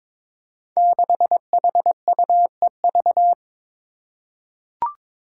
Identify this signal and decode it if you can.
Morse code, heard as a keyed tone, about 22 wpm, 700 Hz. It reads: 6HUEV
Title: Five Challenging Letter-Number Combinations 22wpm